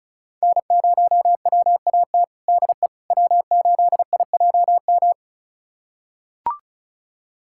Morse code, heard as a keyed tone, about 36 wpm, 700 Hz. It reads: N0WAT DE W8IJM
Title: USA Callsign DE Callsign - 5 Characters - 35wpm